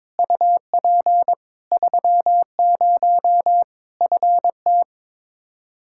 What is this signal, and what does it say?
Morse code, heard as a keyed tone, about 22 wpm, 700 Hz. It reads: UP 30 FT
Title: K1USN SST QSO Practice 22wpm